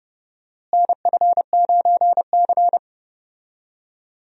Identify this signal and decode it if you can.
Morse code, heard as a keyed tone, about 30 wpm, 700 Hz. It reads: NF9C
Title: USA Call Signs - 4 Characters - Repeated 3x - 30wpm